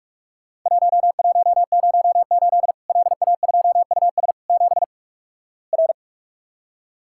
Morse code, heard as a keyed tone, about 44 wpm, 700 Hz. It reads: 1108 LA2US 7 R
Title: Worked All Europe DX Contest - QTC Exchange - 45wpm